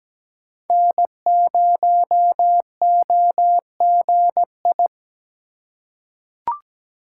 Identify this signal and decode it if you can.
Morse code, heard as a keyed tone, about 17 wpm, 700 Hz. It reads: N0OGI
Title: USA Call Signs - 5 Characters - 3x Character Spacing - 17wpm